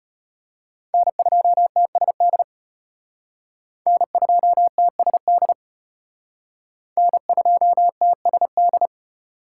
Morse code, heard as a keyed tone, about 32 wpm, 700 Hz. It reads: N2TSD N2TSD N2TSD